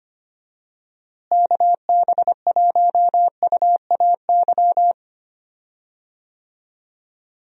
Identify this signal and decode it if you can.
Morse code, heard as a keyed tone, about 25 wpm, 700 Hz. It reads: KB1UAY
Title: USA Call Signs - 6 Characters - Repeated 2x - 25wpm